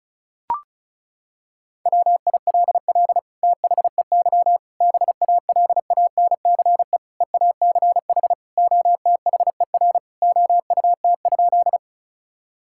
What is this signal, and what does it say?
Morse code, heard as a keyed tone, about 36 wpm, 700 Hz. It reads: WILL THEY BALANCE EACH OTHER OUT?
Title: Sentences from Top 1000 Words - Review Part 6 - 35wpm